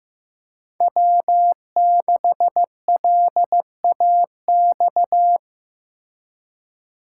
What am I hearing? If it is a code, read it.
Morse code, heard as a keyed tone, about 16 wpm, 700 Hz. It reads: W6LAX
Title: K1USN SST QSO Practice - 15wpm